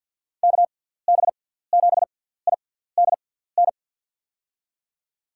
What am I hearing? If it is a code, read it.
Morse code, heard as a keyed tone, about 48 wpm, spaced as for 17 wpm, 700 Hz. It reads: KB7IDN
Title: USA Call Signs - 6 Characters - 6x Character Spacing - 50wpm